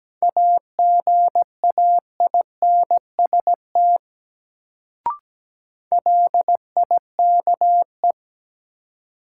Morse code, heard as a keyed tone, about 17 wpm, 700 Hz. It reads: AGAINST LIKE
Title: Sets of 2 Words - New Word Drill - 17wpm